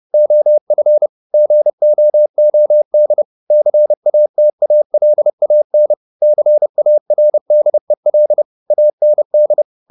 Morse code, heard as a keyed tone, about 30 wpm, 600 Hz. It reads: OF GOOD CATALAN CARDEL AND
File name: the old man and the sea / the old man and the sea 1 30wpm